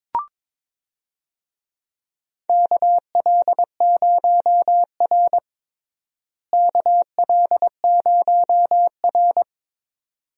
Morse code, heard as a keyed tone, about 22 wpm, 700 Hz. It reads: KL0R KL0R
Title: USA Call Signs - 4 Characters - Repeated 2x - 22wpm